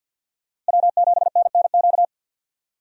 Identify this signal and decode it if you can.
Morse code, heard as a keyed tone, about 47 wpm, 700 Hz. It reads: W7NNQ
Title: USA Call Signs - 5 Characters - Repeated 2x - 50wpm